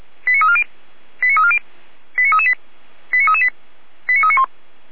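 Five-tone selective call (ZVEI), heard as quick runs of five tones, tones of about 70 ms. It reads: 89340 89340 89338 89338 89391
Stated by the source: ZVEI five-tone call received by SDR.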